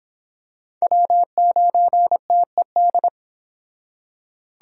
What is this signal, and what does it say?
Morse code, heard as a keyed tone, about 26 wpm, 700 Hz. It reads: W9TED